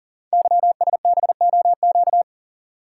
Morse code, heard as a keyed tone, about 39 wpm, 690 Hz. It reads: YSBOQ
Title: Code Groups - 5 Letters - 40wpm